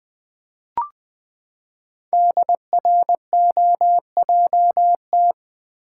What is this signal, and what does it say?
Morse code, heard as a keyed tone, about 20 wpm, 700 Hz. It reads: DROJT
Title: Code Groups - 5 Letters - 20wpm